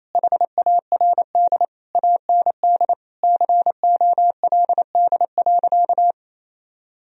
Morse code, heard as a keyed tone, about 28 wpm, 700 Hz. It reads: HARD AND COLD.
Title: Sentences from Top 1000 Words - Review Part 1 - 28wpm